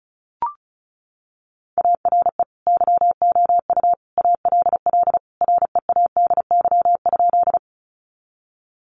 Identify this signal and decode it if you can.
Morse code, heard as a keyed tone, about 34 wpm, 700 Hz. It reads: ARE YOU ALL READY?
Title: Sentences from Top 400 Words 35wpm